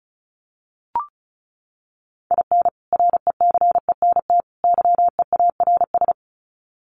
Morse code, heard as a keyed tone, about 34 wpm, 700 Hz. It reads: IN RECENT YEARS
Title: Sets of 3 Words 35wpm